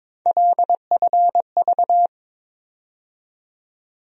Morse code, heard as a keyed tone, about 21 wpm, 700 Hz. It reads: LFV
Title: USA Call Sign Postfixes - 3 Letters 22wpm